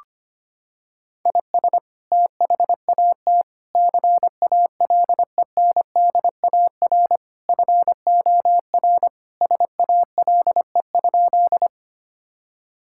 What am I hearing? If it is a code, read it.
Morse code, heard as a keyed tone, about 25 wpm, 700 Hz. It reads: IS THAT CALENDAR FOR SALE?
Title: Sentences from Top 1600 Words 25wpm